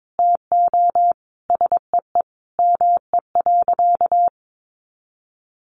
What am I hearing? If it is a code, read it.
Morse code, heard as a keyed tone, about 22 wpm, 700 Hz. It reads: TO SEE ME.
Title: Sentences from Top 1100 Words 22wpm